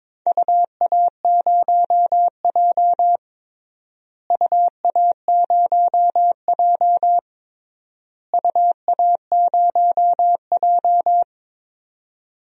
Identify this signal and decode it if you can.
Morse code, heard as a keyed tone, about 22 wpm, 700 Hz. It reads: UA0J UA0J UA0J
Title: DXCC-100 Call Signs - 4 Characters - Repeated 3x - 22wpm